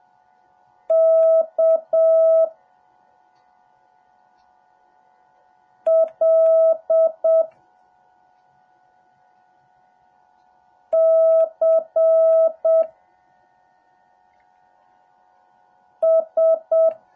Morse code, heard as a keyed tone, about 7 wpm, 650 Hz. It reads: K L C S
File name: jh8cbh / 2021cw / cw08a